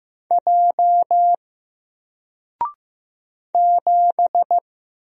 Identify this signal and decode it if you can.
Morse code, heard as a keyed tone, about 15 wpm, 700 Hz. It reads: J 7
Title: Single Letter-Number 15wpm